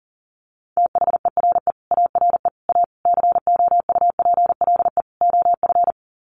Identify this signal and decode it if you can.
Morse code, heard as a keyed tone, about 40 wpm, 700 Hz. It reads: THERE ARE A COUPLE OF
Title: Sets of 5 Words Encore 40wpm